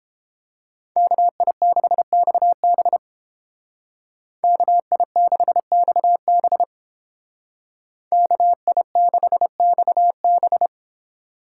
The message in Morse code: KI6XB KI6XB KI6XB

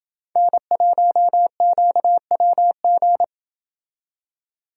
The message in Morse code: N1QWG

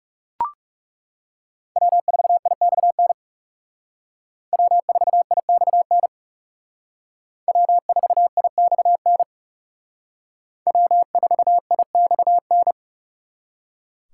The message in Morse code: W4IXN W4IXN W4IXN W4IXN

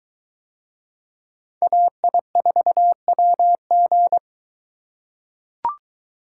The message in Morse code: AI4WG

